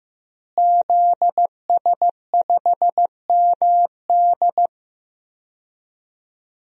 Morse code, heard as a keyed tone, about 15 wpm, 700 Hz. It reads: ZS5MD
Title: DXCC-100 Call Signs - 5 Characters - Repeated 2x - 15wpm